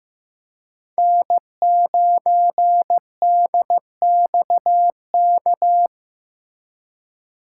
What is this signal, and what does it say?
Morse code, heard as a keyed tone, about 16 wpm, 700 Hz. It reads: N9DXK
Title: USA Call Signs - 5 Characters - 4x Character Spacing - 15wpm